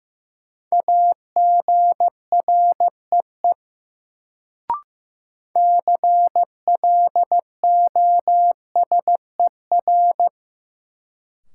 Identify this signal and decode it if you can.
Morse code, heard as a keyed tone, about 15 wpm, 700 Hz. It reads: AGREE CLOSER
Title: Sets of 5 Words Encore - New Word Drill - 15wpm